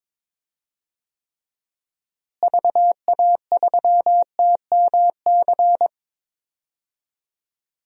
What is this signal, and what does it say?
Morse code, heard as a keyed tone, about 22 wpm, 700 Hz. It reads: VA3TMC
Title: DXCC-100 Call Signs - 6 Characters - Repeated 2x - 22wpm